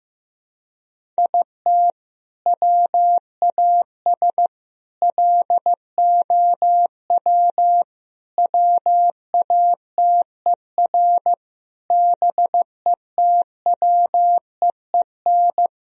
Morse code, heard as a keyed tone, about 15 wpm, 700 Hz. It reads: IT WAS LOW WATER BETWEEN
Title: Sentences from Top 1000 Words - Review Part 2 - 15wpm